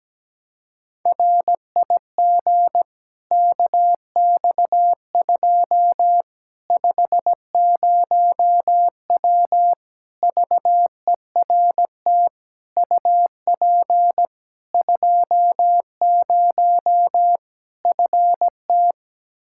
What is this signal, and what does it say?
Morse code, heard as a keyed tone, about 17 wpm, 700 Hz. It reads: RIG KX2 50W VERT UP 20 FT
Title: Practice QSO - Exchange 2 - Info - 17wpm